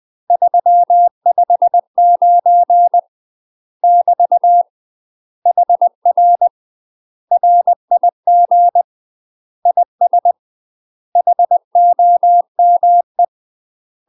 Morse code, heard as a keyed tone, about 20 wpm, 700 Hz. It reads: 359 <BT> HR RIG IS HOME
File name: UK MASTER QSOs TRAINING / 20WPM / UK MASTER QSOs 0014